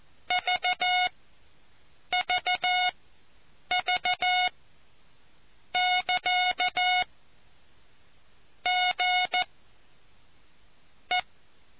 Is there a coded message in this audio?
Yes: Morse code from 0.3 to 11.2 s, heard as a keyed tone, about 14 wpm, spaced as for 6 wpm, 730 Hz. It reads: VVV<KA> GE